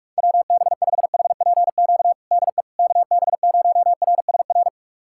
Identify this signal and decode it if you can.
Morse code, heard as a keyed tone, about 46 wpm, 700 Hz. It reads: WB5HPQ DE KB0RSR
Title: USA Callsign DE Callsign - 6 Characters - 45wpm